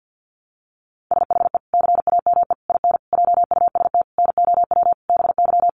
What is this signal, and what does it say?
Morse code, heard as a keyed tone, about 49 wpm, 700 Hz. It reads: SHE CAME IN JUST NOW BY
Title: Sentences from Top 300 Words 50wpm